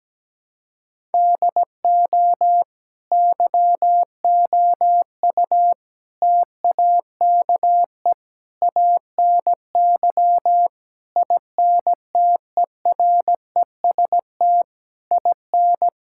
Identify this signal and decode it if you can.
Morse code, heard as a keyed tone, about 17 wpm, 700 Hz. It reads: DO YOU TAKE ANY INTEREST IN
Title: Sentences from Top 1000 Words - Review Part 6 - 17wpm